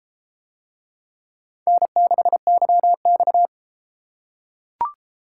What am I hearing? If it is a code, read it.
Morse code, heard as a keyed tone, about 32 wpm, 700 Hz. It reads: N6YX